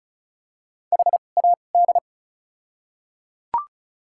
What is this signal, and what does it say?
Morse code, heard as a keyed tone, about 34 wpm, spaced as for 25 wpm, 700 Hz. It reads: HAD
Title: 2x Character Spacing - Top 300 Words - 35wpm